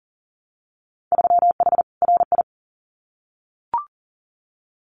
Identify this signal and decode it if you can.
Morse code, heard as a keyed tone, about 40 wpm, 700 Hz. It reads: 3H RI
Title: Winter Field Day Exchanges Encore 40wpm